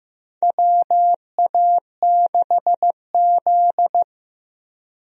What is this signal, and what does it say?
Morse code, heard as a keyed tone, about 15 wpm, 700 Hz. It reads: WA6Z
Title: USA Call Signs - 4 Characters - Repeated 3x - 15wpm